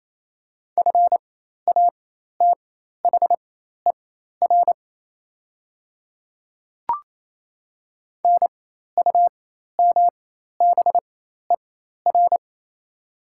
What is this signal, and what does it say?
Morse code, heard as a keyed tone, about 27 wpm, spaced as for 13 wpm, 690 Hz. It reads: FATHER NUMBER
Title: 4x Character Spacing - Top 200 Words - 28wpm